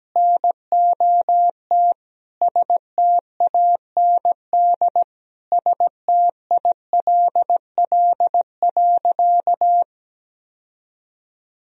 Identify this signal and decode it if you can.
Morse code, heard as a keyed tone, about 17 wpm, 700 Hz. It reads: NOT STAND STILL.
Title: Sentences from Top 1000 Words - Review Part 1 - 17wpm